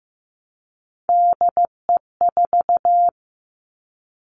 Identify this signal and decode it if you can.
Morse code, heard as a keyed tone, about 15 wpm, 700 Hz. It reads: DE4
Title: Three Challenging Letter-Number Combinations 15wpm